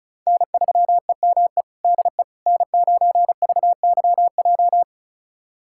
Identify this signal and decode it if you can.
Morse code, heard as a keyed tone, about 34 wpm, 700 Hz. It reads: N3EME DE N9VYJ
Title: USA Callsign DE Callsign - 5 Characters - 35wpm